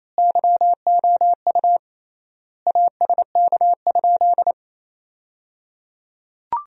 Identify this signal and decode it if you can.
Morse code, heard as a keyed tone, about 28 wpm, 700 Hz. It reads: YOU ASK?
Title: 3x Word-Spacing - Sentences from Top 200 Words - 28wpm